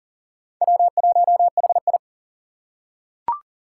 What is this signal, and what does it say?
Morse code, heard as a keyed tone, about 40 wpm, 700 Hz. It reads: W1HI
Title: Four Challenging Letter-Number Combinations 40wpm